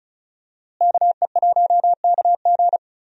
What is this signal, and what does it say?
Morse code, heard as a keyed tone, about 34 wpm, 700 Hz. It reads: KE1KG
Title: USA Call Signs - 5 Characters - Repeated 2x - 35wpm